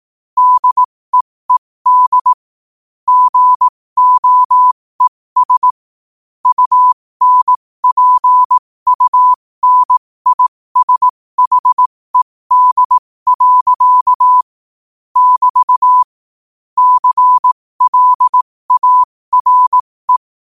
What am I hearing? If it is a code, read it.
Morse code, heard as a keyed tone, about 18 wpm, spaced as for 15 wpm, 1000 Hz. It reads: DEED GOES UNPUNISHED. <BT> CLARE
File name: qotdfiles / 15wpm